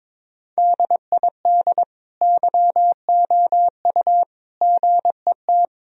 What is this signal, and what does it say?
Morse code, heard as a keyed tone, about 22 wpm, 700 Hz. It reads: DID YOU GET
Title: Sentences from Top 1000 Words - Review Part 3 - 22wpm